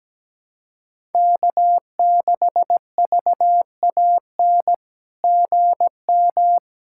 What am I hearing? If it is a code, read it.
Morse code, heard as a keyed tone, about 17 wpm, 700 Hz. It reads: K6VAN GM
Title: SOTA QSO Practice 17wpm